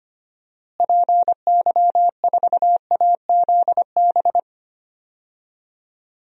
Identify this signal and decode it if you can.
Morse code, heard as a keyed tone, about 25 wpm, 700 Hz. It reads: PY4AZB
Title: DXCC-100 Call Signs - 6 Characters - Repeated 3x - 25wpm